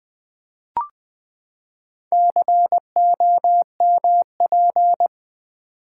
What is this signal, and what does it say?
Morse code, heard as a keyed tone, about 20 wpm, 700 Hz. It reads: COMP
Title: Build A Word - 4 Letters At A Time - 11-Letter Words - 20wpm